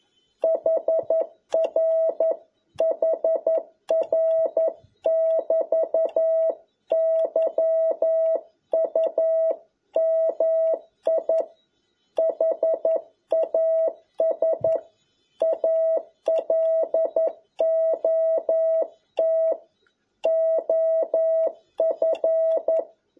Morse code, heard as a keyed tone, about 11 wpm, 650 Hz. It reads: HR HR <BT> YUMI HAS A LOT OF
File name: jh8cbh / 2021cw